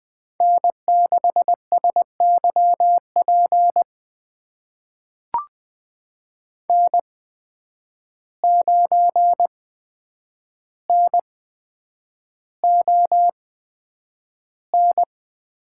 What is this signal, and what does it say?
Morse code, heard as a keyed tone, about 20 wpm, spaced as for 9 wpm, 700 Hz. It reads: N6SYP N9NON
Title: USA Call Signs - 5 Characters - 8x Character Spacing - 20wpm